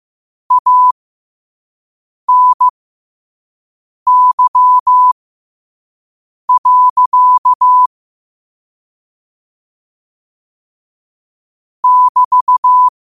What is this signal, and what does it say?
Morse code, heard as a keyed tone, about 15 wpm, spaced as for 5 wpm, 1000 Hz. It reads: ANY. <BT>
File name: qotdfiles / 5wpm